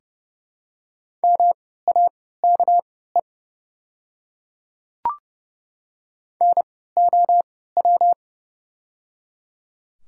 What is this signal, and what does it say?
Morse code, heard as a keyed tone, about 31 wpm, spaced as for 17 wpm, 700 Hz. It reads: MAKE NOW
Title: 3x Character Spacing - Top 300 Words - 30wpm